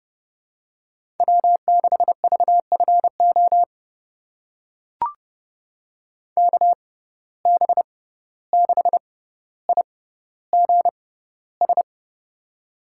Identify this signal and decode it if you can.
Morse code, heard as a keyed tone, about 30 wpm, spaced as for 15 wpm, 700 Hz. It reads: W6VFO KB6IGS
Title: USA Call Signs - 6x Character Spacing - 30wpm